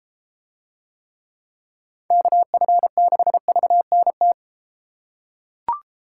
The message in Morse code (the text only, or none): KF6VNT